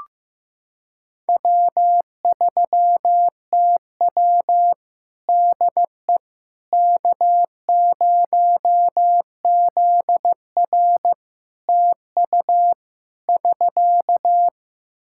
Morse code, heard as a keyed tone, about 16 wpm, 700 Hz. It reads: W3TW DE K0ZR TU <SK>
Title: K1USN SST QSO Practice 15wpm